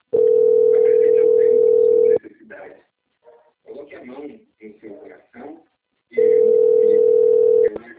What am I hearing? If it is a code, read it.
Ringback tone (North America), 440 + 480 Hz, with speech.
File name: out- - - - - .6126